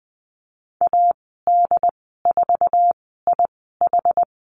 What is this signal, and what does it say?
Morse code, heard as a keyed tone, about 19 wpm, spaced as for 14 wpm, 700 Hz. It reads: AD4IH
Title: 076 Lesson 13g - Letter L - Call Signs 20wpm